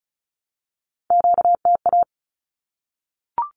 Morse code, heard as a keyed tone, about 34 wpm, 700 Hz. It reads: QTA
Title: Q-Codes 35wpm